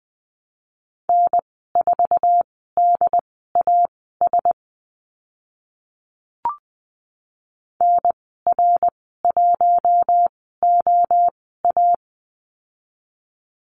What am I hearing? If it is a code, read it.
Morse code, heard as a keyed tone, about 20 wpm, spaced as for 14 wpm, 700 Hz. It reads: N4DAS NR1OA
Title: 069 Lesson 12g - Letter d - Call Signs 20wpm